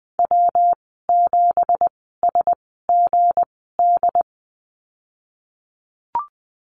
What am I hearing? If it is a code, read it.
Morse code, heard as a keyed tone, about 20 wpm, spaced as for 15 wpm, 700 Hz. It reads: W7SGD
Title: 186 Lesson 30g - Letter b - Call Signs 20wpm